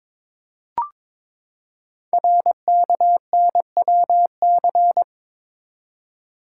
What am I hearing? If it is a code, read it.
Morse code, heard as a keyed tone, about 22 wpm, 700 Hz. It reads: RKNWC
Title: Code Groups - 5 Letters - 22wpm